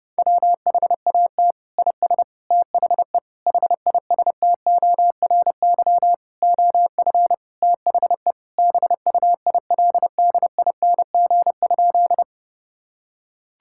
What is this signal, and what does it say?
Morse code, heard as a keyed tone, about 30 wpm, 700 Hz. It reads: WHAT IS THE HISTORY OF THE BUILDING?
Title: Sentences from Top 1000 Words - Review Part 5 - 30wpm